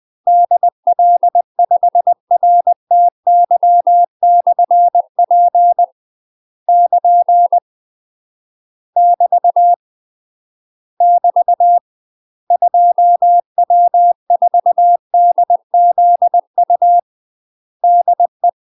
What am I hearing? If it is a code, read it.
Morse code, heard as a keyed tone, about 20 wpm, 700 Hz. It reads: DL5RTY/P <KN> <BT> <BT> 2W4DZU DE
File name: UK MASTER QSOs TRAINING / 20WPM / UK MASTER QSOs 0041